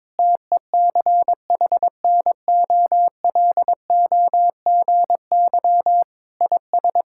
Morse code, heard as a keyed tone, about 22 wpm, 700 Hz. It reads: TECHNOLOGY IS
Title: Sentences from Top 1000 Words - Review Part 5 - 22wpm